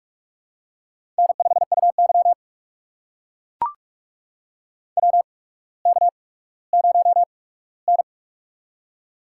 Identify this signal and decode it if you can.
Morse code, heard as a keyed tone, about 44 wpm, spaced as for 19 wpm, 700 Hz. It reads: N5UY WK0N
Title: USA Call Signs - 4 Characters - 8x Character Spacing - 45wpm